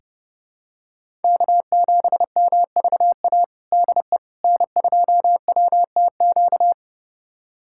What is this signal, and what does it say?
Morse code, heard as a keyed tone, about 30 wpm, 700 Hz. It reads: K7MVA DE N2WTQ
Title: USA Callsign DE Callsign - 5 Characters - 30wpm